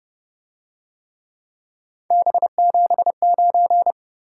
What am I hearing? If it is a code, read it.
Morse code, heard as a keyed tone, about 27 wpm, 700 Hz. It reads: B79